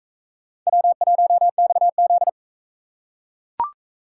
Morse code, heard as a keyed tone, about 38 wpm, 700 Hz. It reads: W1XZ